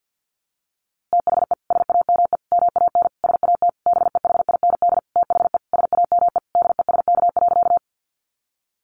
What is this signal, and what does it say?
Morse code, heard as a keyed tone, about 50 wpm, 700 Hz. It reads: THE SAME MAN SAT BEHIND THE SAME DESK.
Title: A Sound Of Thunder 50wpm